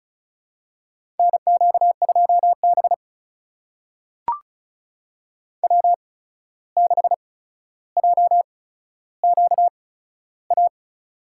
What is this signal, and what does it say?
Morse code, heard as a keyed tone, about 36 wpm, spaced as for 14 wpm, 700 Hz. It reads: NQ2B W6JQA
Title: USA Call Signs - 8x Character Spacing - 35wpm